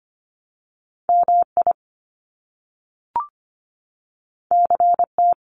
Morse code, heard as a keyed tone, about 25 wpm, 700 Hz. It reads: MI CT
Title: US States 25wpm